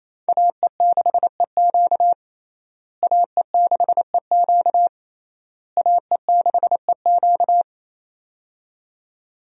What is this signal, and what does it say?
Morse code, heard as a keyed tone, about 27 wpm, 700 Hz. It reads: AE6EQ AE6EQ AE6EQ
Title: USA Call Signs - 5 Characters - Repeated 3x - 28wpm